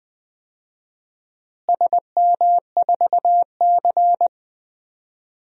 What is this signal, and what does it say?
Morse code, heard as a keyed tone, about 21 wpm, 700 Hz. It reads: SM4C
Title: DXCC-100 Call Signs - 4 Characters - Repeated 2x - 20wpm